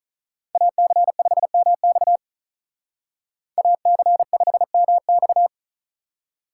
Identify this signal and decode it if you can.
Morse code, heard as a keyed tone, about 42 wpm, 700 Hz. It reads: AC5MX AC5MX
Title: Speed Racing - USA 2x2 Call Signs 35wpm